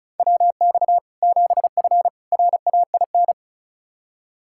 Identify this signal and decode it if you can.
Morse code, heard as a keyed tone, about 36 wpm, 700 Hz. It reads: WX 7F RAIN
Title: K1USN SST QSO Practice 35wpm